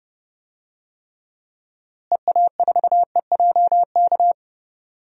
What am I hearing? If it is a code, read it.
Morse code, heard as a keyed tone, about 32 wpm, 700 Hz. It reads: EA4EJK
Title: DXCC-100 Call Signs - 6 Characters - Repeated 2x - 30wpm